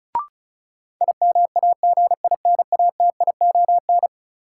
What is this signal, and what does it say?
Morse code, heard as a keyed tone, about 36 wpm, 700 Hz. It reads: IMAGINATION
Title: Sentences from Top 1300 Words - New Word Drill - 35wpm